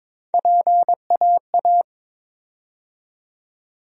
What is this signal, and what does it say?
Morse code, heard as a keyed tone, about 22 wpm, 700 Hz. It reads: PAA
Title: USA Call Sign Postfixes - 3 Letters 22wpm